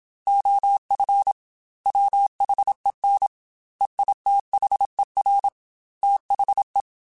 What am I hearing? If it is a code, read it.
Morse code, heard as a keyed tone, about 27 wpm, 800 Hz. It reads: OF WHEN EITHER THE
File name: Technical Subjects / Technical Subjects-mp3s / AND Gate @25